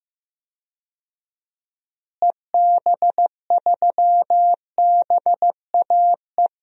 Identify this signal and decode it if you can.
Morse code, heard as a keyed tone, about 15 wpm, 700 Hz. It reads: EB3BAE
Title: DXCC-100 Call Signs - 6 Characters - Repeated 3x - 15wpm